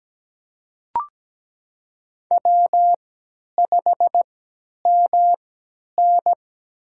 Morse code, heard as a keyed tone, about 17 wpm, spaced as for 10 wpm, 700 Hz. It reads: W5MN
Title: USA Call Signs - 4 Characters - 3x Character Spacing - 17wpm